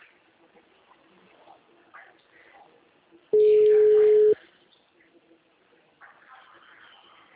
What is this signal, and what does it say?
Ringback tone (Europe, 425 Hz).
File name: out- -2005- - - .71184